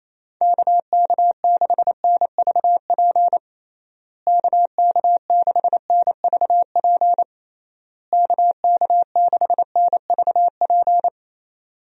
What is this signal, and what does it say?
Morse code, heard as a keyed tone, about 28 wpm, 700 Hz. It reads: KK6NVP KK6NVP KK6NVP
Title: USA Call Signs - 6 Characters - Repeated 3x - 28wpm